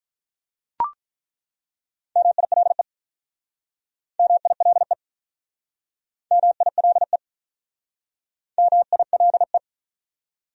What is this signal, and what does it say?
Morse code, heard as a keyed tone, about 47 wpm, 700 Hz. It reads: MILE MILE MILE MILE